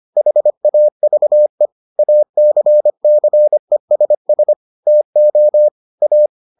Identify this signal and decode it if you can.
Morse code, heard as a keyed tone, about 25 wpm, 600 Hz. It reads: HAVE ACCESS TO A